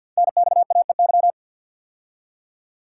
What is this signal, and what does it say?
Morse code, heard as a keyed tone, about 49 wpm, 700 Hz. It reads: NXAEY
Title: Code Groups - 5 Letters - 50wpm